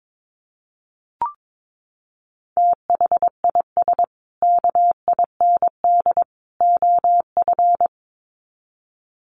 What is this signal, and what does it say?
Morse code, heard as a keyed tone, about 22 wpm, 700 Hz. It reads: THIS KIND OF
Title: Sets of 3 Words 22wpm